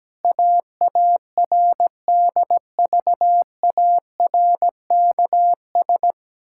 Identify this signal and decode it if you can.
Morse code, heard as a keyed tone, about 17 wpm, 700 Hz. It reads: AARDVARKS
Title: Pangram Challenge v2 - 17wpm